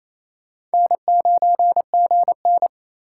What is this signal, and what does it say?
Morse code, heard as a keyed tone, about 34 wpm, 700 Hz. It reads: N9GN
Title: Speed Racing - USA 1x2 Call Signs 28wpm